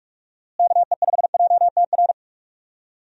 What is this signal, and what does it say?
Morse code, heard as a keyed tone, about 46 wpm, 700 Hz. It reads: KE5JTR
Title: USA Call Signs - 6 Characters - Repeated 3x - 45wpm